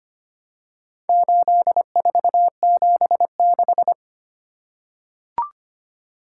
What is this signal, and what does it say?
Morse code, heard as a keyed tone, about 25 wpm, 700 Hz. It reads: 8476
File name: practice / 25wpm / Four Letter-Number Combinations - 25wpm